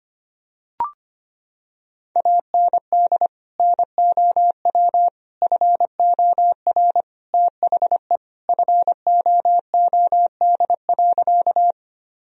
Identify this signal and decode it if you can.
Morse code, heard as a keyed tone, about 25 wpm, 700 Hz. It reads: AND NOW FOR THE FOOD.
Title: Sentences from Top 1000 Words - Review Part 1 - 25wpm